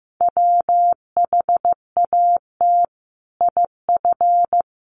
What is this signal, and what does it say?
Morse code, heard as a keyed tone, about 15 wpm, 700 Hz. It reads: WHAT IF
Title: Sentences from Top 1100 Words 15wpm